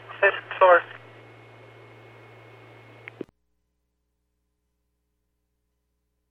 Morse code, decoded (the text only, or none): none